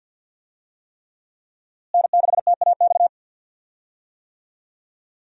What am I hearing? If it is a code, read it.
Morse code, heard as a keyed tone, about 50 wpm, 700 Hz. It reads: N6TAX
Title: USA Call Signs - 5 Characters - Repeated 3x - 50wpm